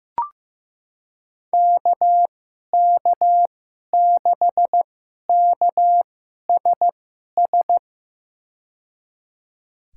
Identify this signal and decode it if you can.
Morse code, heard as a keyed tone, about 15 wpm, spaced as for 11 wpm, 700 Hz. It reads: KK6KSS
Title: USA Call Signs - 6 Characters - 2x Character Spacing - 15wpm